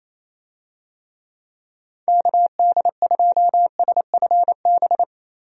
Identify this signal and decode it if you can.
Morse code, heard as a keyed tone, about 28 wpm, 700 Hz. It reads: KD2SFB
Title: USA Call Signs - 6 Characters - 28wpm